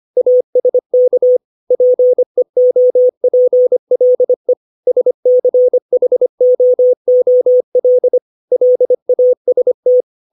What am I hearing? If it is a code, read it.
Morse code, heard as a keyed tone, about 25 wpm, 500 Hz. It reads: ASK PEOPLE SCHOOL LAST
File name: 632 words / 632 Words 25WPM